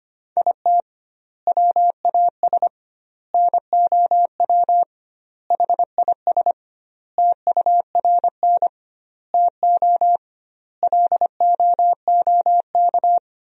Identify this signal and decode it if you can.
Morse code, heard as a keyed tone, about 25 wpm, 700 Hz. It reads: IT WAS NOW HIS TURN TO LOOK
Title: 2x Word-Spacing - Sentences from Top 200 Words - 25wpm